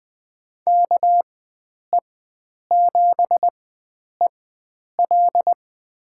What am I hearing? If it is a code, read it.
Morse code, heard as a keyed tone, about 21 wpm, spaced as for 9 wpm, 700 Hz. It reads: KE7EL
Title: USA Call Signs - 5 Characters - 4x Character Spacing - 20wpm